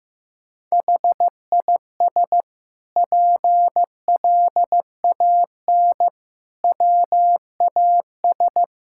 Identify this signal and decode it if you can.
Morse code, heard as a keyed tone, about 15 wpm, 700 Hz. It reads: HIS PLAN WAS
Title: Sentences from Top 1000 Words - Review Part 6 - 15wpm